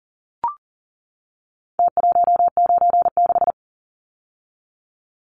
Morse code, heard as a keyed tone, about 40 wpm, 700 Hz. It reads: T196
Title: Numbers - Contest Style - 4-Digit 40wpm